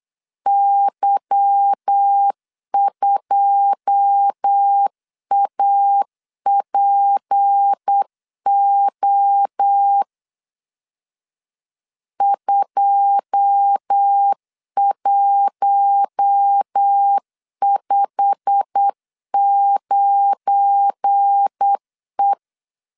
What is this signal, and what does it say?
Morse code, heard as a keyed tone, about 8 wpm, 800 Hz. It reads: Y2APO 2159E
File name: LEZIONI / 9 2 8wpm 3437